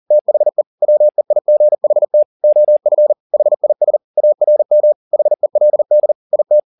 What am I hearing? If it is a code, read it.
Morse code, heard as a keyed tone, about 40 wpm, 600 Hz. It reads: THE WEIGHT OF HIS ARM HELD IT